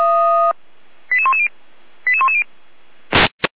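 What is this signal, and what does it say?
Five-tone selective call (ZVEI), heard as quick runs of five tones, tones of about 80 ms. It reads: 88110 88110
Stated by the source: ZVEI five-tone selective call 88192 received by SDR, from the Feuerwehr Orlamünde fire brigade server.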